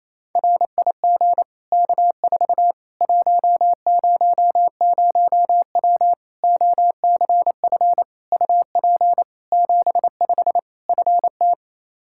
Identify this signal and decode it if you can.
Morse code, heard as a keyed tone, about 28 wpm, 700 Hz. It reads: RIG K4 100W OCF UP 75 FT
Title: Practice QSO - Exchange 2 - Info - 28wpm